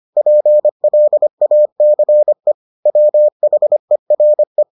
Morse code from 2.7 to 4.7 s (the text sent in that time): WHERE